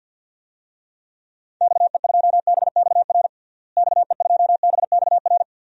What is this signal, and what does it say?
Morse code, heard as a keyed tone, about 52 wpm, 700 Hz. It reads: XE2BXR XE2BXR
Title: DXCC-100 Call Signs - 6 Characters - Repeated 3x - 50wpm